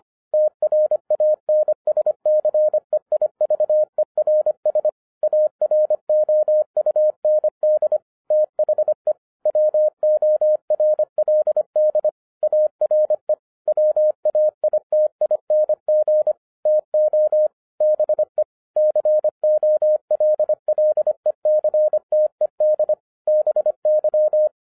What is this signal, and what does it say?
Morse code, heard as a keyed tone, about 25 wpm, 610 Hz. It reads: TRANSCEIVERS AROUND THE WORLD ARE WAITING TO BE COLLECTED BY
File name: 2011-07-mp3 / n24 QRP OTAKU 25wpm